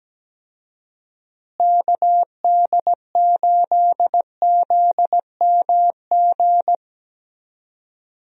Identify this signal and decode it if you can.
Morse code, heard as a keyed tone, about 17 wpm, 700 Hz. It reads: KD8ZMG
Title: USA Call Signs - 6 Characters - Repeated 2x - 17wpm